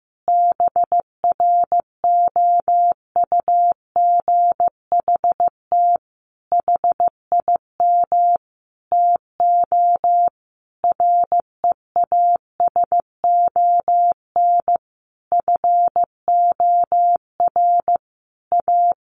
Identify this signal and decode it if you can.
Morse code, heard as a keyed tone, about 15 wpm, 700 Hz. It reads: BROUGHT HIM TO REASON FOR A
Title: Sentences from Top 700 Words 15wpm